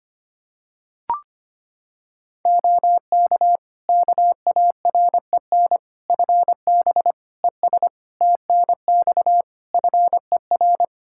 Morse code, heard as a keyed tone, about 25 wpm, 700 Hz. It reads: OK KAREN FB ES TNX FER
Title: Practice QSO - Exchange 2 - Info - 25wpm